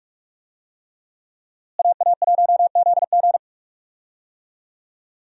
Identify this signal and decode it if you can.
Morse code, heard as a keyed tone, about 42 wpm, 700 Hz. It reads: AA1ZG
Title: USA Call Signs - 5 Characters - Repeated 3x - 45wpm